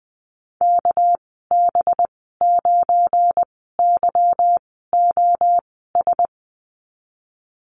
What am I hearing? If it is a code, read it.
Morse code, heard as a keyed tone, about 20 wpm, spaced as for 15 wpm, 700 Hz. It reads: KB9YOS
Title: 219 Lesson 35g - Number 0 - Call Signs 20wpm